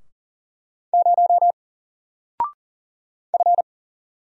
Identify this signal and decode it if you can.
Morse code, heard as a keyed tone, about 39 wpm, 700 Hz. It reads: 0 F
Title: Single Character (Letter:Number:Punctuation:Prosign) Alt 40wpm